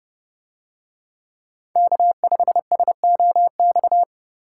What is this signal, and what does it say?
Morse code, heard as a keyed tone, about 29 wpm, 700 Hz. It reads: K5SOX